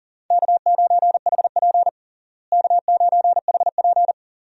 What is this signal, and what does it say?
Morse code, heard as a keyed tone, about 40 wpm, 700 Hz. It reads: K9HP K9HP
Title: USA Call Signs - 4 Characters - Repeated 2x - 40wpm